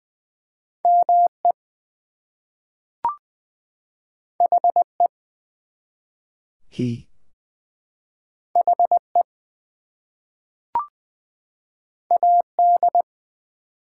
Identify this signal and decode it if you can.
Morse code, heard as a keyed tone, about 20 wpm, 700 Hz. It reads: ME HE HE AD
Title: Two Letter Words - 20wpm